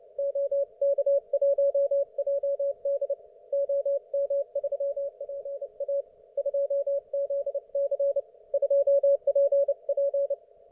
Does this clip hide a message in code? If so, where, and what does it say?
Yes: Morse code from 0.2 to 10.3 s, heard as a keyed tone, about 29 wpm, 560 Hz. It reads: OK1JD OM3PA 2ZC 2PP